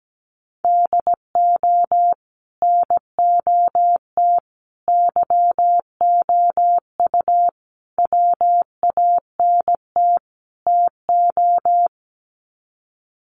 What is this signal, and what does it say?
Morse code, heard as a keyed tone, about 17 wpm, 700 Hz. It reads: DO NOT YOU WANT TO
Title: Sets of 5 Words Encore2 17wpm